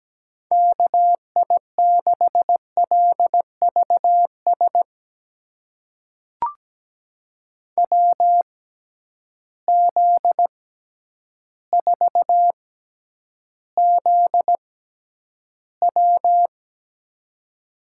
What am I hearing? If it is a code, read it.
Morse code, heard as a keyed tone, about 17 wpm, spaced as for 9 wpm, 700 Hz. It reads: KI6LVS WZ4ZW
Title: USA Call Signs - 6x Character Spacing - 17wpm